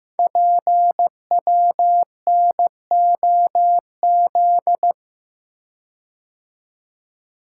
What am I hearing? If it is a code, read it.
Morse code, heard as a keyed tone, about 15 wpm, 700 Hz. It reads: PWNOZ
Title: Code Groups - 5 Letters - 15wpm